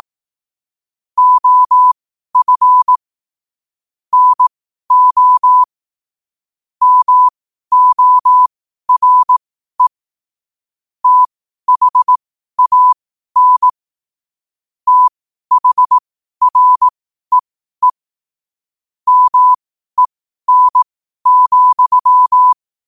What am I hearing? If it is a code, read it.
Morse code, heard as a keyed tone, about 18 wpm, spaced as for 12 wpm, 1000 Hz. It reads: OF NO MORE THAN THREE MEN,
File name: qotdfiles / 12wpm